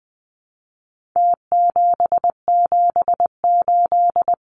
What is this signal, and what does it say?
Morse code, heard as a keyed tone, about 20 wpm, 700 Hz. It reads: T778
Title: Numbers - Contest Style - 4-Digit 20wpm